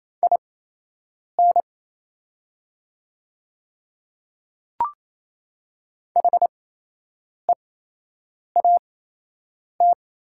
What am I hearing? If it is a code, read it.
Morse code, heard as a keyed tone, about 28 wpm, spaced as for 8 wpm, 690 Hz. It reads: IN HEAT
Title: 8x Character Spacing - Top 500 Words - 28wpm